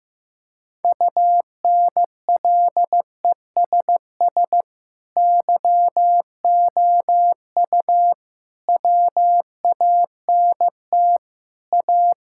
Morse code, heard as a keyed tone, about 15 wpm, 700 Hz. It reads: UNLESS YOU WANT A
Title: Sentences from Top 1600 Words 15wpm